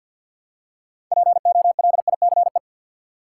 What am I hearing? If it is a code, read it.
Morse code, heard as a keyed tone, about 53 wpm, 700 Hz. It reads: POLICE
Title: Six Letter Words - 50wpm